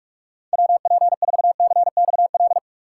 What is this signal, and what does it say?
Morse code, heard as a keyed tone, about 44 wpm, 700 Hz. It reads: WP4CXL
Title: USA Call Signs - 6 Characters - 6x Character Spacing - 45wpm